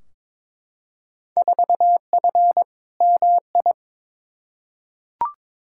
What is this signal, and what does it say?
Morse code, heard as a keyed tone, about 22 wpm, 700 Hz. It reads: 4F MI
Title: Field Day Exchanges - 22wpm